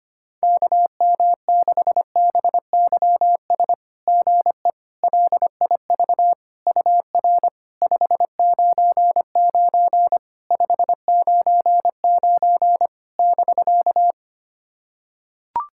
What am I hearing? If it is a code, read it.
Morse code, heard as a keyed tone, about 26 wpm, 700 Hz. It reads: KM6BYS GE LIV UR 599 599 <BK>
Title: SOTA QSO Practice 25wpm